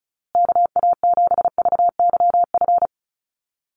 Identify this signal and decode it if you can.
Morse code, heard as a keyed tone, about 34 wpm, 700 Hz. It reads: KA7VYF
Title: USA Call Signs Repeated 3x 35wpm